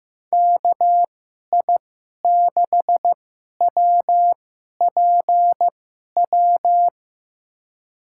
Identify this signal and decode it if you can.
Morse code, heard as a keyed tone, about 15 wpm, spaced as for 11 wpm, 700 Hz. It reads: KI6WPW
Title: USA Call Signs - 6 Characters - 2x Character Spacing - 15wpm